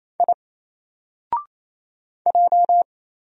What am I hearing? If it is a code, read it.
Morse code, heard as a keyed tone, about 27 wpm, 700 Hz. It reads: I J
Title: Single Letter-Number 28wpm